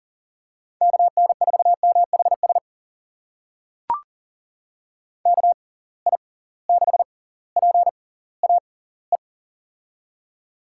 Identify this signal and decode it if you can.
Morse code, heard as a keyed tone, about 40 wpm, spaced as for 20 wpm, 700 Hz. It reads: KN4MHS KI6PAE
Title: USA Call Signs - 6 Characters - 6x Character Spacing - 40wpm